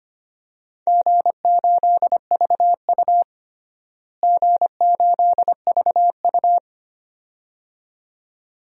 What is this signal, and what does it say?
Morse code, heard as a keyed tone, about 25 wpm, 700 Hz. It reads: G8VU G8VU
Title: DXCC-100 Call Signs - 4 Characters - Repeated 2x - 25wpm